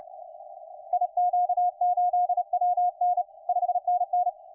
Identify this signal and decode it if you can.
Morse code, heard as a keyed tone, about 30 wpm, 720 Hz, with radio noise. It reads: IQ8WN 5NN
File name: 5T5OK 10CW